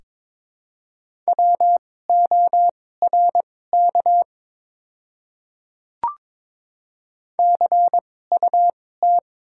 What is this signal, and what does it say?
Morse code, heard as a keyed tone, about 22 wpm, spaced as for 16 wpm, 700 Hz. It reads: WORK CUT
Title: 2x Character Spacing - Top 300 Words - 22wpm